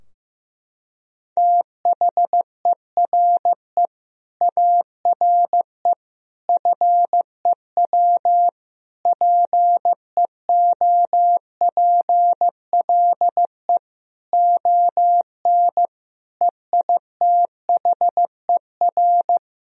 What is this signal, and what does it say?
Morse code, heard as a keyed tone, about 15 wpm, 700 Hz. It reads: THERE ARE FEW PEOPLE ON EITHER
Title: Sentences from Top 1000 Words - Review Part 4 - 15wpm